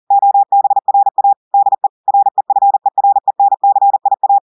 Morse code, heard as a keyed tone, about 40 wpm, 800 Hz. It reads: OBRA DE REFERENCIA
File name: audios / portugais / portugais texte en clair 1 vitesse 40 mots minutes